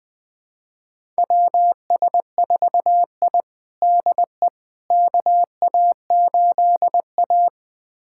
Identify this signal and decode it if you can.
Morse code, heard as a keyed tone, about 20 wpm, 700 Hz. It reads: WS4I DE KA8A
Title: USA Callsign DE Callsign - 4 Characters - 20wpm